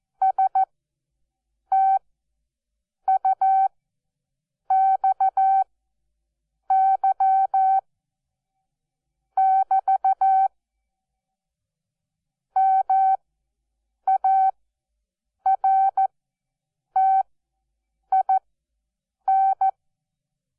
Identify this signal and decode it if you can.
Morse code, heard as a keyed tone, about 14 wpm, spaced as for 6 wpm, 770 Hz. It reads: STUXY<BT> MARTIN
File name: Onlinelehrgang / morsen / DJ4UF Morsen12